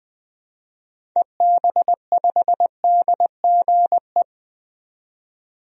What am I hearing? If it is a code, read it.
Morse code, heard as a keyed tone, about 20 wpm, 700 Hz. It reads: EB5DGE
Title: DXCC-100 Call Signs - 6 Characters - 20wpm